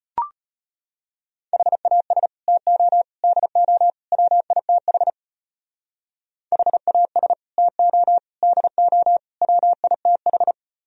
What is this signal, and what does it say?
Morse code, heard as a keyed tone, about 30 wpm, 700 Hz. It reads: HAS TO DO WITH HAS TO DO WITH
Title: Speed-Racing - Sets of 4 Words 25wpm